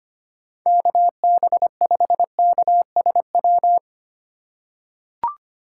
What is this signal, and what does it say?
Morse code, heard as a keyed tone, about 25 wpm, 700 Hz. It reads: KB5KSW
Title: USA Call Signs - 6 Characters - 2x Character Spacing - 25wpm